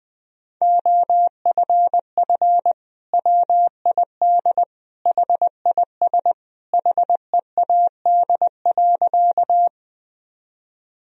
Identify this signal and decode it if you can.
Morse code, heard as a keyed tone, about 20 wpm, 700 Hz. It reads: OFF WID HIS HEAD.
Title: Sentences from Top 300 Words and Abbreviations 20wpm